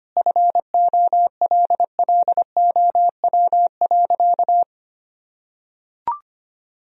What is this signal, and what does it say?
Morse code, heard as a keyed tone, about 25 wpm, 700 Hz. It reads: FOLLOW.
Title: Sentences from Top 1000 Words - Review Part 2 - 25wpm